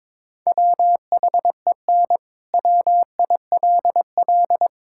Morse code, heard as a keyed tone, about 22 wpm, 700 Hz. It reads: WHEN WILL
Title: Sentences from Top 1000 Words - Review Part 5 - 22wpm